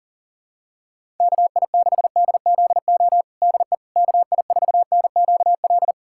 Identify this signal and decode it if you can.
Morse code, heard as a keyed tone, about 37 wpm, 700 Hz. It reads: KI6DZO DE KI4NQL
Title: USA Callsign DE Callsign 40wpm